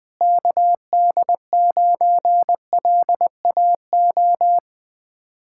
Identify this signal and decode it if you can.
Morse code, heard as a keyed tone, about 21 wpm, 700 Hz. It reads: KD9LAO
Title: Speed-Racing Common QSO Phrases - 20wpm